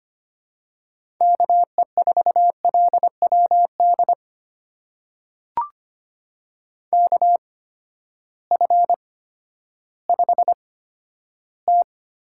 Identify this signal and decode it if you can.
Morse code, heard as a keyed tone, about 25 wpm, spaced as for 12 wpm, 700 Hz. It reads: KE4LWD KF5T
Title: USA Call Signs - 6 Characters - 8x Character Spacing - 25wpm